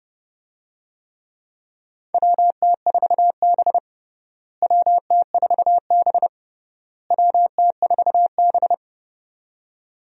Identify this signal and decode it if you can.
Morse code, heard as a keyed tone, about 30 wpm, 700 Hz. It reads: WT4B WT4B WT4B
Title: USA Call Signs - 4 Characters - Repeated 3x - 30wpm